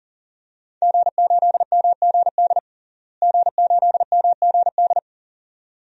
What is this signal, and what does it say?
Morse code, heard as a keyed tone, about 40 wpm, 700 Hz. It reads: G8MGD G8MGD
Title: DXCC-100 Call Signs - 5 Characters - Repeated 2x - 40wpm